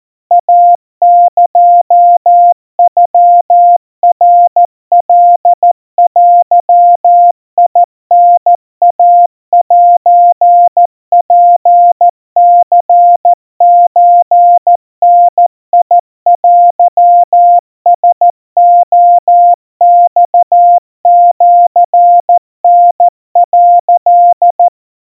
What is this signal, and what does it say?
Morse code (Japanese Wabun code), heard as a keyed tone, about 14 wpm, 700 Hz. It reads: イエノナカデタイセツニソダテラレマシタ」
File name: kawa / Morse sample / kamyu 50 cpm 700 Hz Crispy